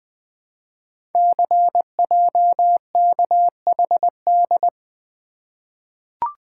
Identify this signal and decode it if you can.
Morse code, heard as a keyed tone, about 20 wpm, 700 Hz. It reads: CJKHD
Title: Code Groups - 5 Letter:Numbers - 20wpm